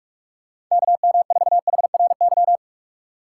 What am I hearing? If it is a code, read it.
Morse code, heard as a keyed tone, about 46 wpm, 700 Hz. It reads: KM4HRY
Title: USA Call Signs - 6 Characters - 2x Character Spacing - 45wpm